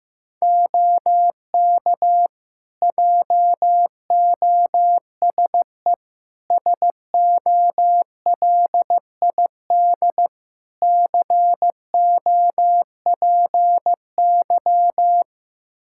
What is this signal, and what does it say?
Morse code, heard as a keyed tone, about 15 wpm, 700 Hz. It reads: OK JOSE SOLID COPY
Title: K1USN SST QSO Practice 15wpm